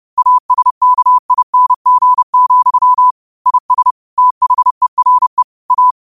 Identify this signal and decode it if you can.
Morse code, heard as a keyed tone, about 30 wpm, 1000 Hz. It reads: ASKING, IS THERE A
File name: qotdfiles / 30wpm